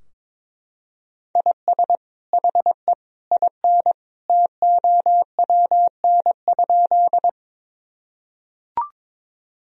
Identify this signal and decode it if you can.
Morse code, heard as a keyed tone, about 22 wpm, 700 Hz. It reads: IS HE IN TOWN?
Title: Sentences from Top 1000 Words - Review Part 2 - 22wpm